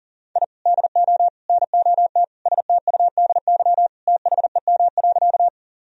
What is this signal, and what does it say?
Morse code, heard as a keyed tone, about 37 wpm, 700 Hz. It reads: I DO NOT STUDY THEM.
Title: Sentences from Top 1000 Words - Review Part 1 - 40wpm